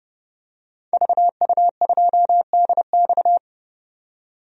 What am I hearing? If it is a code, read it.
Morse code, heard as a keyed tone, about 30 wpm, 700 Hz. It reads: VU2DX
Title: DXCC-100 Call Signs - 5 Characters - 30wpm